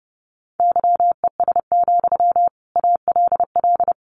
Morse code, heard as a keyed tone, about 28 wpm, 700 Hz. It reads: YES, ALL
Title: Sentences from Top 500 Words 30wpm